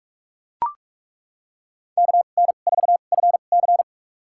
Morse code, heard as a keyed tone, about 41 wpm, spaced as for 31 wpm, 700 Hz. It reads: KN4FC
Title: USA Call Signs - 2x Character Spacing - 45wpm